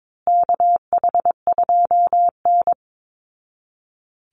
Morse code, heard as a keyed tone, about 22 wpm, 700 Hz. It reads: KH2N
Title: Call Signs Repeated 3x - 22wpm